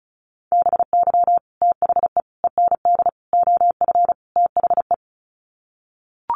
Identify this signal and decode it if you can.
Morse code, heard as a keyed tone, about 34 wpm, 700 Hz. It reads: BY THE END OF THE
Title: Sets of 5 Words 35wpm